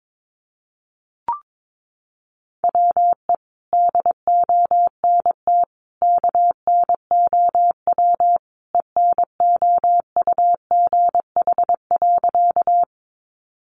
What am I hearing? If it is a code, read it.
Morse code, heard as a keyed tone, about 22 wpm, 700 Hz. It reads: WE DONT KNOW ENOUGH.
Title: Sentences from Top 400 Words 22wpm